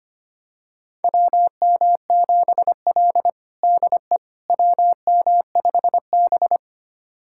Morse code, heard as a keyed tone, about 25 wpm, 700 Hz. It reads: WM7L DE WM5B
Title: USA Callsign DE Callsign - 4 Characters - 25wpm